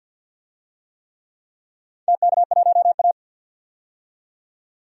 Morse code, heard as a keyed tone, about 46 wpm, 700 Hz. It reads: TK1A
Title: DXCC-100 Call Signs - 4 Characters - 50wpm